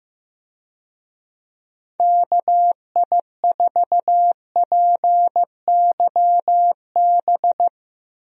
Morse code, heard as a keyed tone, about 15 wpm, 700 Hz. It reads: KI4PYB
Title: USA Call Signs - 6 Characters - Repeated 3x - 15wpm